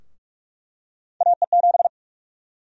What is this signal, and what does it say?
Morse code, heard as a keyed tone, about 47 wpm, 700 Hz. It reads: AE7
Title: USA Call Sign Prefixes - 45wpm